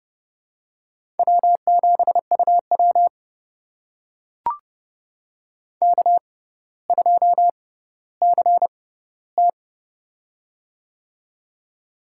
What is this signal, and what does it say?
Morse code, heard as a keyed tone, about 32 wpm, spaced as for 16 wpm, 700 Hz. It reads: W7UW K2CT
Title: USA Call Signs - 4 Characters - 6x Character Spacing - 30wpm